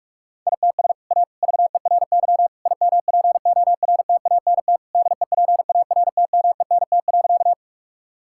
Morse code, heard as a keyed tone, about 46 wpm, 700 Hz. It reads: ITS A VERY IMPORTANT DEPARTMENT.
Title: Sentences from Top 1000 Words - Review Part 5 - 45wpm